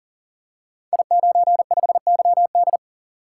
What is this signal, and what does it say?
Morse code, heard as a keyed tone, about 39 wpm, 700 Hz. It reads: I95YD
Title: Code Groups - 5 Letter:Numbers - 40wpm